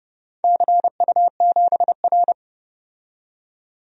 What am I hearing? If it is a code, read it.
Morse code, heard as a keyed tone, about 31 wpm, 700 Hz. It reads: CU7R
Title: DXCC-100 Call Signs - 4 Characters - 30wpm